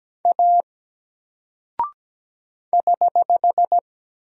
Morse code, heard as a keyed tone, about 17 wpm, 700 Hz. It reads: A <HH>
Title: Single Character (Letter:Number:Punctuation:Prosign) 17wpm